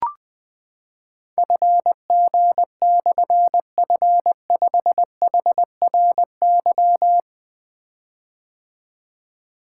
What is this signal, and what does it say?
Morse code, heard as a keyed tone, about 20 wpm, 700 Hz. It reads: FG/F5HRY
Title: Difficult Call Signs 20wpm